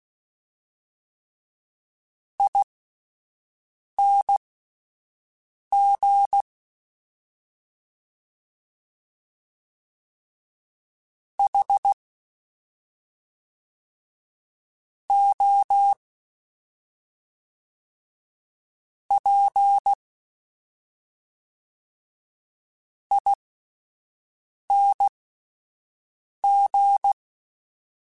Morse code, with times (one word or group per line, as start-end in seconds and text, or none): 2.4-6.4 ING
11.4-11.9 H
15.1-15.9 O
19.1-19.9 P
23.1-27.1 ING